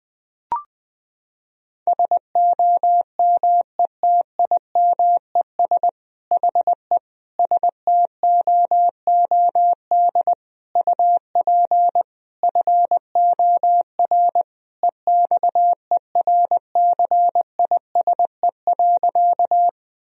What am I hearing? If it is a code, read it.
Morse code, heard as a keyed tone, about 21 wpm, 700 Hz. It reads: SOMETIMES HE STOOD UP FOR EXERCISE.
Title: Sentences from Top 1000 Words - Review Part 4 - 20wpm